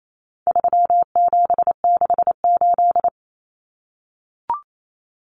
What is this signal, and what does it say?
Morse code, heard as a keyed tone, about 28 wpm, 700 Hz. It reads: 3768
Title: Numbers - 4-Digit 28wpm